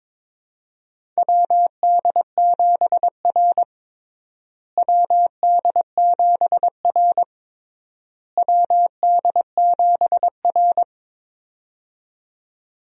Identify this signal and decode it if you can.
Morse code, heard as a keyed tone, about 22 wpm, 700 Hz. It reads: WD7R WD7R WD7R
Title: USA Call Signs - 4 Characters - Repeated 3x - 22wpm